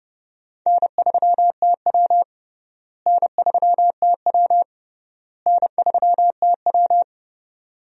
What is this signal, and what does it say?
Morse code, heard as a keyed tone, about 32 wpm, 700 Hz. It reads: N3TW N3TW N3TW
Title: USA Call Signs - 4 Characters - Repeated 3x - 30wpm